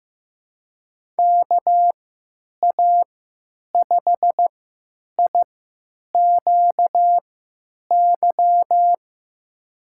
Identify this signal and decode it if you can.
Morse code, heard as a keyed tone, about 15 wpm, spaced as for 9 wpm, 700 Hz. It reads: KA5IQY
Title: USA Call Signs - 6 Characters - 3x Character Spacing - 15wpm